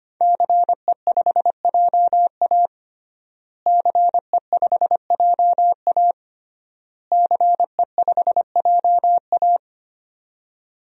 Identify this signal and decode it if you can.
Morse code, heard as a keyed tone, about 25 wpm, 700 Hz. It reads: CE5JA CE5JA CE5JA
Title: DXCC-100 Call Signs - 5 Characters - Repeated 3x - 25wpm